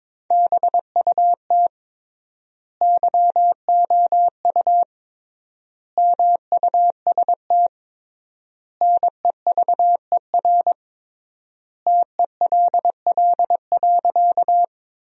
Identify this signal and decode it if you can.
Morse code, heard as a keyed tone, about 22 wpm, 700 Hz. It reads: BUT YOU MUST NEVER TELL.
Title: 3x Word-Spacing - Sentences from Top 300 Words - 22wpm